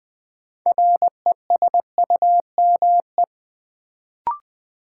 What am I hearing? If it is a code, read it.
Morse code, heard as a keyed tone, about 20 wpm, 700 Hz. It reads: RESUME
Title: Build A Word - 3 Letters At A Time - 6-Letter Words - 20wpm